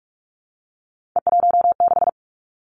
Morse code, heard as a keyed tone, about 46 wpm, 700 Hz. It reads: E16
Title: Three Challenging Letter-Number Combinations 45wpm